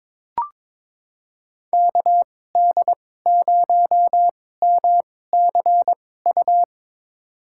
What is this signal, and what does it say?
Morse code, heard as a keyed tone, about 22 wpm, spaced as for 16 wpm, 700 Hz. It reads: KD0MCU
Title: USA Call Signs - 2x Character Spacing - 22wpm